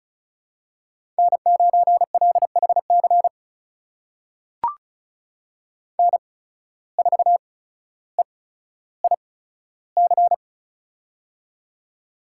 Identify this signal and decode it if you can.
Morse code, heard as a keyed tone, about 36 wpm, spaced as for 15 wpm, 700 Hz. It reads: N9LHC N4EIC
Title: USA Call Signs - 5 Characters - 8x Character Spacing - 35wpm